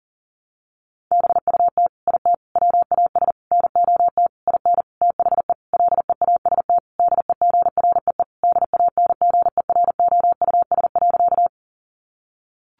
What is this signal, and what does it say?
Morse code, heard as a keyed tone, about 40 wpm, 700 Hz. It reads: BUT IT WAS NOT IN THE LEAST DEGREE DANGEROUS.
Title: Sentences from Top 1500 Words 40wpm